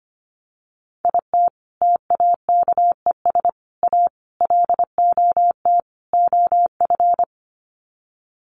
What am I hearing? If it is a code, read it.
Morse code, heard as a keyed tone, about 25 wpm, 700 Hz. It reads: IT TAKES A LOT OF
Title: Sets of 5 Words Encore 25wpm